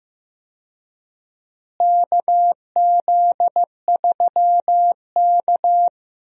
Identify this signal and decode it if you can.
Morse code, heard as a keyed tone, about 15 wpm, 700 Hz. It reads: KZ3K
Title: USA Call Signs - 4 Characters - Repeated 3x - 15wpm